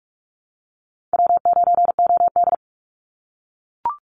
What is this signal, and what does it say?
Morse code, heard as a keyed tone, about 44 wpm, 700 Hz. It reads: W9OD
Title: Easy Call Signs - 45wpm